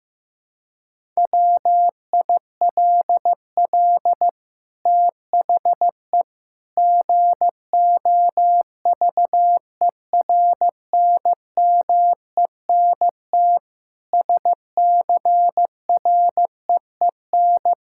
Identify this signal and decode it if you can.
Morse code, heard as a keyed tone, about 15 wpm, 700 Hz. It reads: WILL THE GOVERNMENT SCREEN
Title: Sentences from Top 1000 Words - Review Part 6 - 15wpm